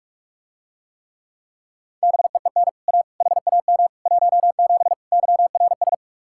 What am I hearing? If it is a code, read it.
Morse code, heard as a keyed tone, about 47 wpm, 700 Hz. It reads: BEEN A HAM 17 YRS